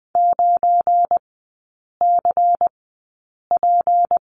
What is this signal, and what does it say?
Morse code, heard as a keyed tone, about 20 wpm, spaced as for 8 wpm, 700 Hz. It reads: 9CP
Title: 172 Lesson 28e - Number 9 - Three Characters 20wpm